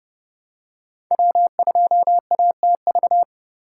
Morse code, heard as a keyed tone, about 31 wpm, 700 Hz. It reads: W2ATV
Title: USA Call Signs - 5 Characters - 4x Character Spacing - 30wpm